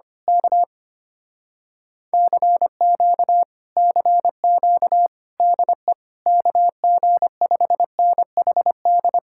Morse code, heard as a keyed tone, about 24 wpm, 700 Hz. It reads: K CQ CQ DE KG5NHD
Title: Speed-Racing Common QSO Phrases - 25wpm